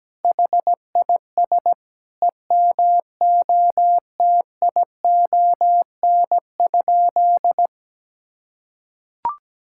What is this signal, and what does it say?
Morse code, heard as a keyed tone, about 17 wpm, 700 Hz. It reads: HIS EMOTION?
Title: Sentences from Top 1600 Words 17wpm